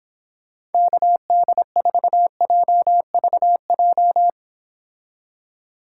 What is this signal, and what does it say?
Morse code, heard as a keyed tone, about 29 wpm, 700 Hz. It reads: KD4JVJ